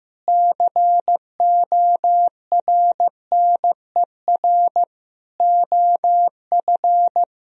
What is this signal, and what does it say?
Morse code, heard as a keyed tone, about 15 wpm, 700 Hz. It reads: CORNER OF
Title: Speed-Racing - Sets of 5 Words 15wpm